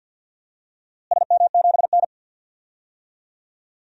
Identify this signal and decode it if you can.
Morse code, heard as a keyed tone, about 49 wpm, 700 Hz. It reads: SM7N
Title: DXCC-100 Call Signs - 4 Characters - Repeated 2x - 50wpm